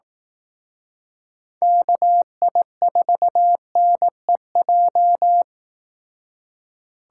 Morse code, heard as a keyed tone, about 18 wpm, 700 Hz. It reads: KI4NEJ
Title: Speed Racing - USA Call Signs 15wpm